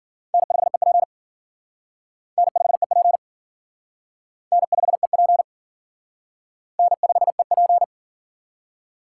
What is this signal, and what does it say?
Morse code, heard as a keyed tone, about 49 wpm, 700 Hz. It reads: N5EP N5EP N5EP N5EP
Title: Speed Racing - USA 1x2 Call Signs 40wpm